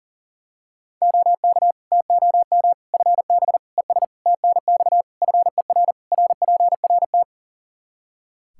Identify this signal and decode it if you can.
Morse code, heard as a keyed tone, about 40 wpm, 700 Hz. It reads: OK TOM FB ES TNX FER RPRT
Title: Practice QSO - Exchange 2 - Info - 40wpm